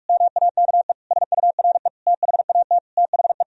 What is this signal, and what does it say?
Morse code, heard as a keyed tone, about 46 wpm, 700 Hz. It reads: MAKE SURE THAT THE